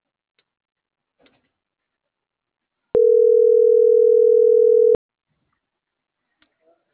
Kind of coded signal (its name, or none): telephone call progress tone